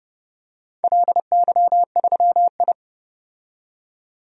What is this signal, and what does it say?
Morse code, heard as a keyed tone, about 30 wpm, 700 Hz. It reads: LY3I
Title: DXCC-100 Call Signs - 4 Characters - 30wpm